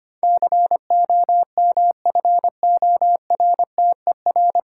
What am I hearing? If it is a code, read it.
Morse code, heard as a keyed tone, about 26 wpm, 700 Hz. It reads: COMFORTER
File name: Easy Words - Speed-Racing with Hints - 25wpm